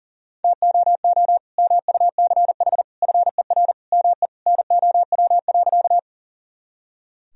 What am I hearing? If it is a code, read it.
Morse code, heard as a keyed tone, about 40 wpm, 700 Hz. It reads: TOO MUCH FER ME NOW.
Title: Sentences from Top 200 Words and Abbreviations 40wpm